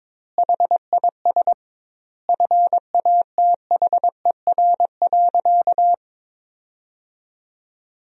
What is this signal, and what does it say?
Morse code, heard as a keyed tone, about 21 wpm, 700 Hz. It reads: HIS FATHER.
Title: 2x Word-Spacing - Sentences from Top 500 Words - 22wpm